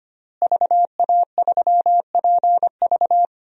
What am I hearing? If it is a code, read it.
Morse code, heard as a keyed tone, about 25 wpm, 700 Hz. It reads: VA3PV
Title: DXCC-100 Call Signs - 5 Characters - Repeated 3x - 25wpm